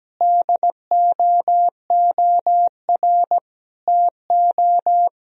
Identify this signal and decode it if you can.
Morse code, heard as a keyed tone, about 17 wpm, 700 Hz. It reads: DOOR TO
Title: Sentences from Top 1000 Words - Review Part 1 - 17wpm